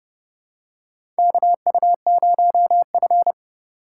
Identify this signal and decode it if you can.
Morse code, heard as a keyed tone, about 30 wpm, 700 Hz. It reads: KU0F
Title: USA Call Signs - 4 Characters - 3x Character Spacing - 30wpm